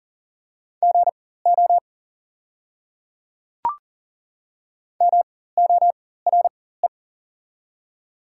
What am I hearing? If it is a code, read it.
Morse code, heard as a keyed tone, about 40 wpm, spaced as for 19 wpm, 700 Hz. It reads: GO MORE
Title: 4x Character Spacing - Top 100 Words - 40wpm